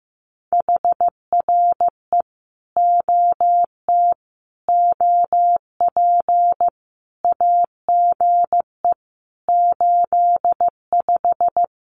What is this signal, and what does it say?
Morse code, heard as a keyed tone, about 15 wpm, 700 Hz. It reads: HRE OT OP AGE 85
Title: Common QSO Phrases 15wpm